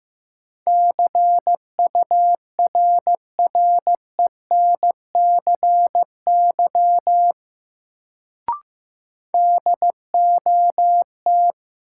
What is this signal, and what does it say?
Morse code, heard as a keyed tone, about 15 wpm, 700 Hz. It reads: CURRENCY DOT
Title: Sentences from Top 1500 Words - New Word Drill - 15wpm